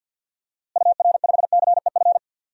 Morse code, heard as a keyed tone, about 52 wpm, 700 Hz. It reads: UR5CEF
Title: DXCC-100 Call Signs - 6 Characters - Repeated 3x - 50wpm